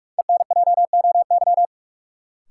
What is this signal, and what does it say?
Morse code, heard as a keyed tone, about 45 wpm, 700 Hz. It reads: ENJOY
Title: Sentences from Top 1000 Words - Review Part 4 - New Word Drill - 45wpm